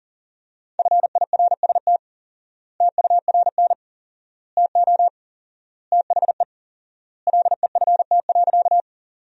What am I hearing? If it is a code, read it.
Morse code, heard as a keyed tone, about 40 wpm, 700 Hz. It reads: FIRST TURN TO THE LEFT.
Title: 4x Word-Spacing - Sentences from Top 300 Words - 40wpm